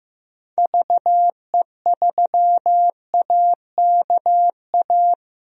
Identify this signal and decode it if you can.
Morse code, heard as a keyed tone, about 15 wpm, 700 Hz. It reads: VE3AKA
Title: DXCC-100 Call Signs - 6 Characters - Repeated 3x - 15wpm